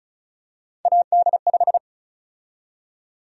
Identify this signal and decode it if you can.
Morse code, heard as a keyed tone, about 36 wpm, 700 Hz. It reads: AD5
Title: USA Call Sign Prefixes - Repeated 2x - 35wpm